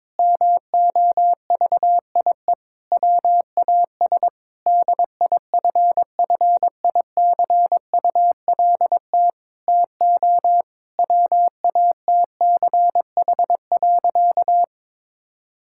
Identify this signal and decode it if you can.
Morse code, heard as a keyed tone, about 22 wpm, 700 Hz. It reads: MOVIE WAS DIFFICULT TO WATCH.
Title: Sentences from Top 1000 Words - Review Part 5 - 22wpm